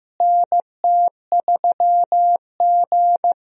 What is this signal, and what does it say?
Morse code, heard as a keyed tone, about 15 wpm, 700 Hz. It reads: NT3G
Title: USA Call Signs - 4 Characters - Repeated 2x - 15wpm